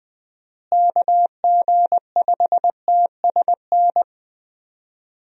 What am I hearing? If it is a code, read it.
Morse code, heard as a keyed tone, about 21 wpm, 700 Hz. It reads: KG5TSN
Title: USA Call Signs - 6 Characters - Repeated 2x - 20wpm